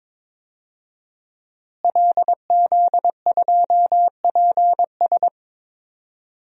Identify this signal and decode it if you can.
Morse code, heard as a keyed tone, about 22 wpm, 700 Hz. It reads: LZ2PS
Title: DXCC-100 Call Signs - 5 Characters - Repeated 2x - 22wpm